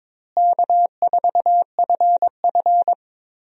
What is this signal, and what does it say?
Morse code, heard as a keyed tone, about 22 wpm, 700 Hz. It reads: K4FF
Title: USA Call Signs - 4 Characters - Repeated 3x - 22wpm